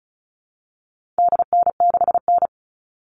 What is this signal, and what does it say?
Morse code, heard as a keyed tone, about 34 wpm, 700 Hz. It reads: DN6N
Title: All Cut Numbers - 4-Digit 35wpm